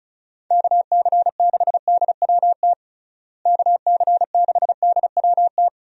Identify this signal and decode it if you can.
Morse code, heard as a keyed tone, about 34 wpm, 700 Hz. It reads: KC6DWT KC6DWT
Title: USA Call Signs - 6 Characters - Repeated 2x - 35wpm